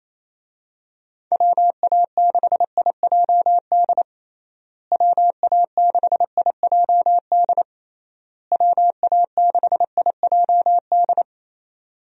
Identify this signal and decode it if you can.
Morse code, heard as a keyed tone, about 28 wpm, 700 Hz. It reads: WA6IJD WA6IJD WA6IJD
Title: USA Call Signs - 6 Characters - Repeated 3x - 28wpm